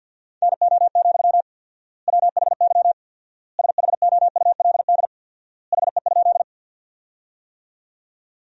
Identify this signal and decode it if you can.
Morse code, heard as a keyed tone, about 53 wpm, 700 Hz. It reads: NO, WHY SHOULD HE?
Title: 4x Word-Spacing - Sentences from Top 300 Words - 50wpm